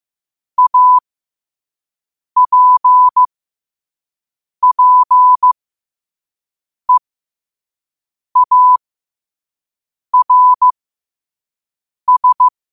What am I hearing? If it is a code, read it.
Morse code, heard as a keyed tone, about 15 wpm, spaced as for 5 wpm, 1000 Hz. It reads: APPEARS
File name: qotdfiles / 5wpm